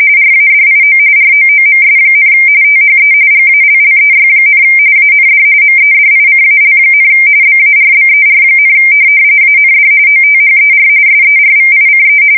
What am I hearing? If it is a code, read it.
RTTY, about 45.45 baud. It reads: RADIO ROOMS, THE GALLEY, AND MEDICAL, BARBER, AND OTHER SERVICE
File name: fsk170 / USS-New-Jersey